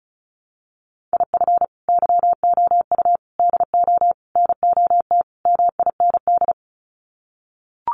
Morse code, heard as a keyed tone, about 34 wpm, 700 Hz. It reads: IF YOU DO NOT MIND
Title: Sets of 5 Words 35wpm